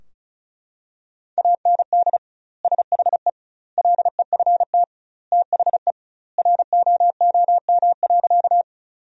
Morse code, heard as a keyed tone, about 36 wpm, 700 Hz. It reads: AND SHE LEFT THE ROOM.
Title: 2x Word-Spacing - Sentences from Top 300 Words - 35wpm